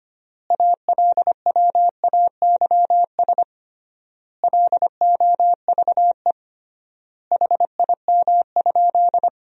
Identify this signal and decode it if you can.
Morse code, heard as a keyed tone, about 25 wpm, 700 Hz. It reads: ALWAYS LOVE HIM?
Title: 3x Word-Spacing - Sentences from Top 400 Words - 25wpm